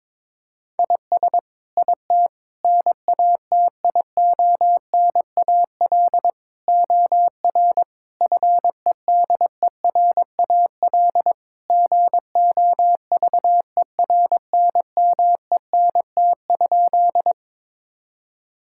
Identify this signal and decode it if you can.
Morse code, heard as a keyed tone, about 22 wpm, 700 Hz. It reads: IS IT NATIONAL OR FEDERAL GOVERNMENT?
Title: Sentences from Top 1000 Words - Review Part 5 - 22wpm